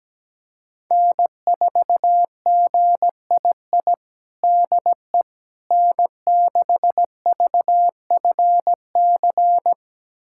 Morse code, heard as a keyed tone, about 17 wpm, 700 Hz. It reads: N4GII DE N6VFC
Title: USA Callsign DE Callsign - 5 Characters - 17wpm